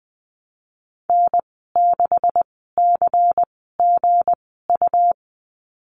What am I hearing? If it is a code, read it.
Morse code, heard as a keyed tone, about 20 wpm, spaced as for 14 wpm, 700 Hz. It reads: N6CGU
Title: 213 Lesson 34g - Number 8 - Call Signs 20wpm